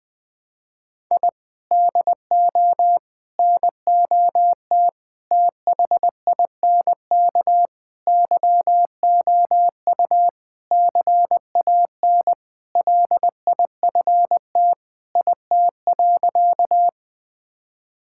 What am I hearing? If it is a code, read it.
Morse code, heard as a keyed tone, about 20 wpm, 700 Hz. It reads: I DO NOT THINK YOU CAN LIFT IT.
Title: Sentences with 5-Word Phrases - 20wpm